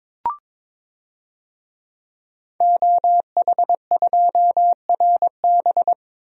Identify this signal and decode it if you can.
Morse code, heard as a keyed tone, about 22 wpm, 700 Hz. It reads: OH2RB
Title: DXCC-100 Call Signs - 5 Characters - Repeated 2x - 22wpm